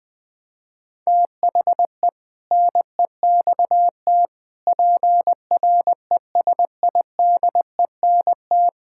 Morse code, heard as a keyed tone, about 20 wpm, 700 Hz. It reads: THE NEXT PRESIDENT
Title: Sentences with 7-Word Phrases - 20wpm